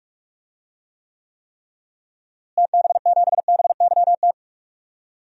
Morse code, heard as a keyed tone, about 46 wpm, 700 Hz. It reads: TB7BYT
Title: DXCC-100 Call Signs - 6 Characters - Repeated 3x - 45wpm